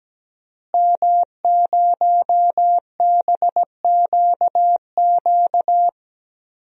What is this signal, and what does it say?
Morse code, heard as a keyed tone, about 17 wpm, 700 Hz. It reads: M0BQQ
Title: DXCC-100 Call Signs - 5 Characters - Repeated 2x - 17wpm